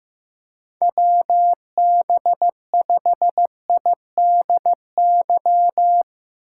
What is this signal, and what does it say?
Morse code, heard as a keyed tone, about 15 wpm, 700 Hz. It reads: WB5IDY
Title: K1USN SST QSO Practice - 15wpm